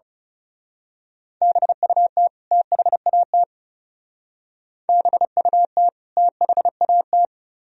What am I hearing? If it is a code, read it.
Morse code, heard as a keyed tone, about 34 wpm, 700 Hz. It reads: BUT THAT BUT THAT
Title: Speed-Racing - Sets of 2 Words 30wpm